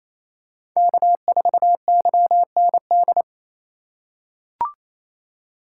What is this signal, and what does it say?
Morse code, heard as a keyed tone, about 28 wpm, 700 Hz. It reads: K4YND
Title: K1USN SST QSO Practice - 28wpm